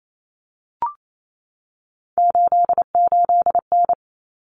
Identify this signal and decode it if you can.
Morse code, heard as a keyed tone, about 28 wpm, 700 Hz. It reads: 88N
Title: Common Cut Numbers - 3-Digit 28wpm